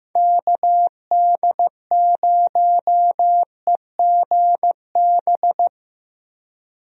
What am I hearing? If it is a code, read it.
Morse code, heard as a keyed tone, about 15 wpm, 700 Hz. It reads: KD0EGB
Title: USA Call Signs - 6 Characters - Repeated 2x - 15wpm